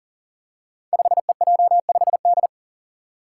Morse code, heard as a keyed tone, about 40 wpm, 700 Hz. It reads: 5EJ5D
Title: Five Challenging Letter-Number Combinations 40wpm